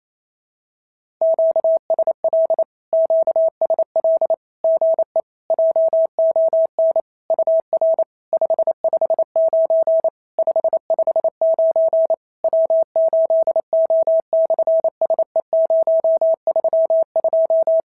Morse code, heard as a keyed tone, about 27 wpm, 650 Hz. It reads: QSL QSL GE JON UR 559 559 W8O/SE032
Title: SOTA QSO Practice 28wpm